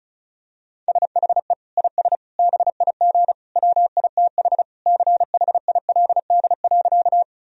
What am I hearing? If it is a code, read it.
Morse code, heard as a keyed tone, about 34 wpm, 700 Hz. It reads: SHE IS BIG WITH CHILD.
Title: Sentences from Top 1000 Words - Review Part 1 - 35wpm